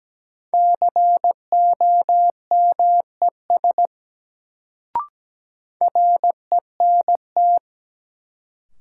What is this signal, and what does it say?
Morse code, heard as a keyed tone, about 17 wpm, 700 Hz. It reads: COMES RENT
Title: Sentences from Top 1000 Words - Review Part 5 - New Word Drill - 17wpm